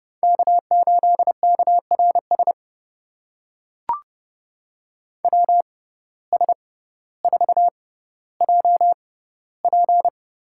Morse code, heard as a keyed tone, about 31 wpm, spaced as for 16 wpm, 700 Hz. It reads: K8KRS WS4JP
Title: USA Call Signs - 5 Characters - 6x Character Spacing - 30wpm